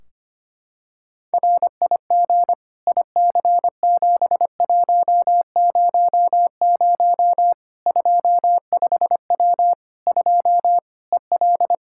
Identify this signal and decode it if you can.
Morse code, heard as a keyed tone, about 25 wpm, 700 Hz. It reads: RIG IC7100 25W 2 EL
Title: K1USN SST QSO Practice 25wpm